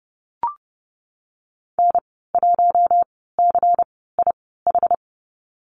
Morse code, heard as a keyed tone, about 30 wpm, spaced as for 17 wpm, 700 Hz. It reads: N1CIH
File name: Lesson 19g - Letter w - Call Signs 30wpm